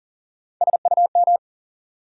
Morse code, heard as a keyed tone, about 40 wpm, 700 Hz. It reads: SUM
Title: Sentences from Top 1000 Words - Review Part 6 - New Word Drill - 40wpm